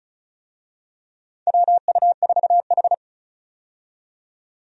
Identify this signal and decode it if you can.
Morse code, heard as a keyed tone, about 36 wpm, 700 Hz. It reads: WU4H
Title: USA Call Signs - 4 Characters - 35wpm